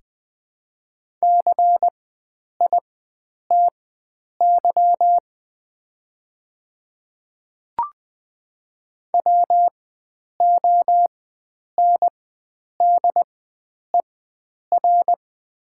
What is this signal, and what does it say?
Morse code, heard as a keyed tone, about 20 wpm, spaced as for 9 wpm, 700 Hz. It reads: CITY WONDER
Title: 4x Character Spacing - Top 500 Words - 20wpm